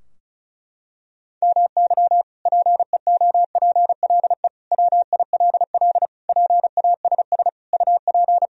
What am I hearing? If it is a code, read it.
Morse code, heard as a keyed tone, about 36 wpm, 700 Hz. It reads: MY PEOPLE WILL PASS UP
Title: Sentences from Top 1000 Words - Review Part 2 - 35wpm